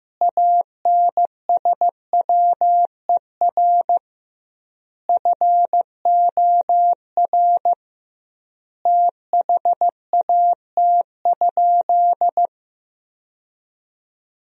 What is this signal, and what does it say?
Morse code, heard as a keyed tone, about 15 wpm, 700 Hz. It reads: ANSWER FOR THAT?
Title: 2x Word-Spacing - Sentences from Top 300 Words - 15wpm